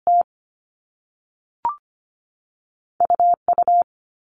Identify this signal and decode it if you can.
Morse code, heard as a keyed tone, about 25 wpm, 700 Hz. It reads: T UU
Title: All Cut Numbers - 2-Digit 25wpm